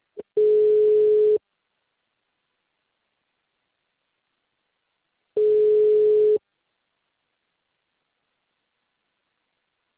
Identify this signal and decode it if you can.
Ringback tone (Europe, 425 Hz).
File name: out- -2012